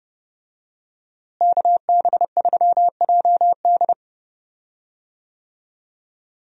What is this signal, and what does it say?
Morse code, heard as a keyed tone, about 30 wpm, 700 Hz. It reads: KB3JD
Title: USA Call Signs - 5 Characters - Repeated 3x - 30wpm